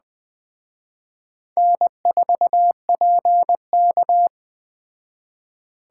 Morse code, heard as a keyed tone, about 20 wpm, 700 Hz. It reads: N4PK
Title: Speed Racing - USA 1x2 Call Signs 17wpm